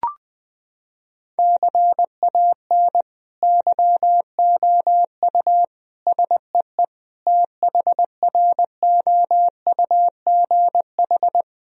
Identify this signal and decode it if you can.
Morse code, heard as a keyed tone, about 20 wpm, 700 Hz. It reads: CAN YOU SEE THROUGH
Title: Sentences from Top 1000 Words - Review Part 1 - 20wpm